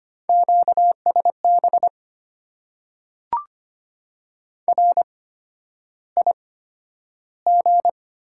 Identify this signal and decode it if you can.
Morse code, heard as a keyed tone, about 25 wpm, spaced as for 11 wpm, 690 Hz. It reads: QSB RIG
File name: Top 90 QSO Elements - 8x Character Spacing - 25wpm